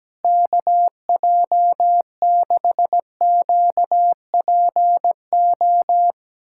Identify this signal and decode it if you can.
Morse code, heard as a keyed tone, about 17 wpm, 700 Hz. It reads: KJ6QPO
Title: USA Callsign DE Callsign 17wpm